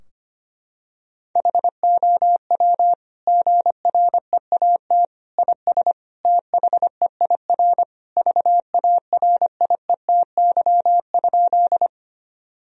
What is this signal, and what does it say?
Morse code, heard as a keyed tone, about 25 wpm, 700 Hz. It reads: HOW GREAT IS THEIR VARIETY?
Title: Sentences from Top 1000 Words - Review Part 4 - 25wpm